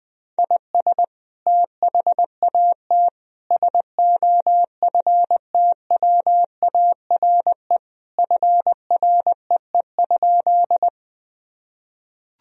Morse code, heard as a keyed tone, about 20 wpm, 700 Hz. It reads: IS THAT SOFTWARE FREE?
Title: Sentences from Top 1000 Words - Review Part 3 - 20wpm